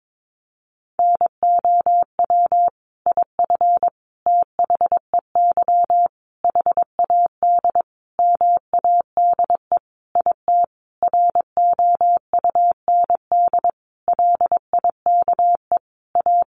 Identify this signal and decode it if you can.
Morse code, heard as a keyed tone, about 22 wpm, 700 Hz. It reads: NOW IF THEY HAD MADE IT ROUND LIKE A
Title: Sentences from Top 500 Words 22wpm